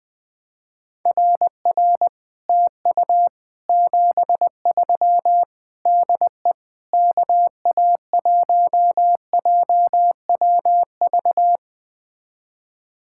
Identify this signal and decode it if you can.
Morse code, heard as a keyed tone, about 20 wpm, 700 Hz. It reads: RR TU 73 DE KA1JWV
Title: Speed-Racing Common QSO Phrases - 17wpm